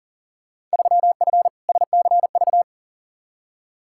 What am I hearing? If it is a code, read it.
Morse code, heard as a keyed tone, about 38 wpm, 700 Hz. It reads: 3F SCV
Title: Field Day Exchanges - 40wpm